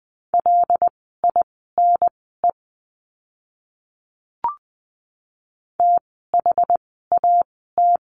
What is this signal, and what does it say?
Morse code, heard as a keyed tone, about 20 wpm, spaced as for 15 wpm, 700 Hz. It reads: LINE THAT
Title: 115 Lesson 19f - Letter w - Words 20wpm